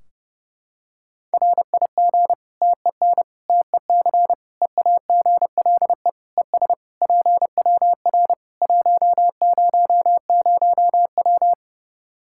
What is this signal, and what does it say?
Morse code, heard as a keyed tone, about 30 wpm, 700 Hz. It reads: RIG TEN TEC EAGLE ES PWR 100W
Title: Practice QSO - Exchange 2 - Info - 30wpm